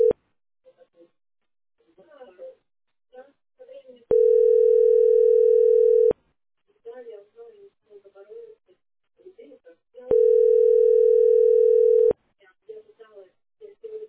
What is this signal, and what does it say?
Ringback tone (North America), 440 + 480 Hz, with speech.
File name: db681ef6-7324-43b6-9084